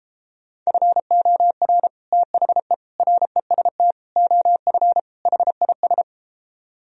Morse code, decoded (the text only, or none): FOR THE REST OF HIS